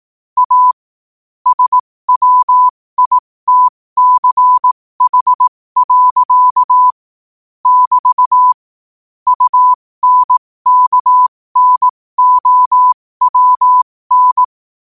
Morse code, heard as a keyed tone, about 18 wpm, spaced as for 15 wpm, 1000 Hz. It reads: A SWITCH. <BT> UNKNOWN
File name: qotdfiles / 15wpm